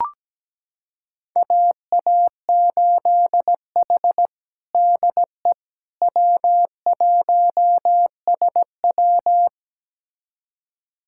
Morse code, heard as a keyed tone, about 17 wpm, 700 Hz. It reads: AA8H DE W1SW
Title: USA Callsign DE Callsign - 4 Characters - 17wpm